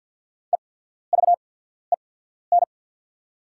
Morse code, heard as a keyed tone, about 48 wpm, spaced as for 14 wpm, 690 Hz. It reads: EVEN
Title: 8x Character Spacing - Top 500 Words - 50wpm